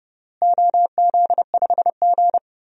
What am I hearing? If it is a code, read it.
Morse code, heard as a keyed tone, about 32 wpm, 700 Hz. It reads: OZ5G
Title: DXCC-100 Call Signs - 4 Characters - Repeated 3x - 30wpm